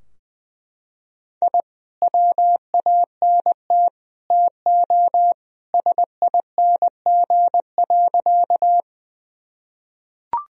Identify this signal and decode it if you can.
Morse code, heard as a keyed tone, about 20 wpm, 700 Hz. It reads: I WANT TO SING.
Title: Sentences from Top 1000 Words - Review Part 2 - 20wpm